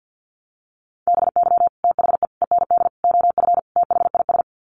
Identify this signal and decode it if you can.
Morse code, heard as a keyed tone, about 50 wpm, 700 Hz. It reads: BY THE END OF THIS
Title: Sets of 5 Words Encore 50wpm